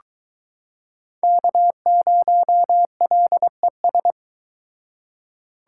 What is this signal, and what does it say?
Morse code, heard as a keyed tone, about 22 wpm, 700 Hz. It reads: K0LES